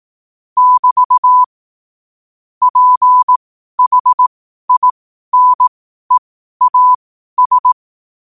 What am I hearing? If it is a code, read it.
Morse code, heard as a keyed tone, about 18 wpm, spaced as for 12 wpm, 1000 Hz. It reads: <BT> PHINEAS
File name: qotdfiles / 12wpm